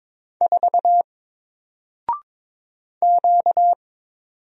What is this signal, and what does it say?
Morse code, heard as a keyed tone, about 22 wpm, 700 Hz. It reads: 4 Q
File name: Single-Letter-Number-22wpm